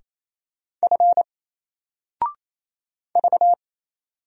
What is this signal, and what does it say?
Morse code, heard as a keyed tone, about 27 wpm, 700 Hz. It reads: F V
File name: Single-Letter-Number-28wpm